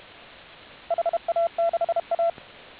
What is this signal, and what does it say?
Morse code, heard as a keyed tone, about 32 wpm, 680 Hz, with radio noise. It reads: HA6A